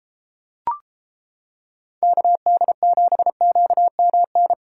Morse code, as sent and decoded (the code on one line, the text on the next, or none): -.- -.. --... --.- -- -.
KD7QMN